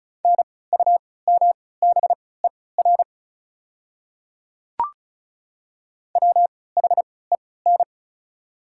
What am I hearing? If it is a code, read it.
Morse code, heard as a keyed tone, about 34 wpm, spaced as for 20 wpm, 690 Hz. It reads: NUMBER WHEN
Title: 3x Character Spacing - Top 100 Words - 35wpm